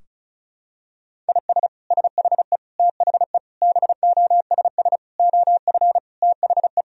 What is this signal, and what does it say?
Morse code, heard as a keyed tone, about 36 wpm, 700 Hz. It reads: IS SHE THE BOSS OF THE
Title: Sentences from Top 1000 Words - Review Part 4 - 35wpm